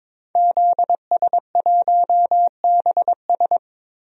Morse code, heard as a keyed tone, about 22 wpm, 700 Hz. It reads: ZS1BS
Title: DXCC-100 Call Signs - 5 Characters - 22wpm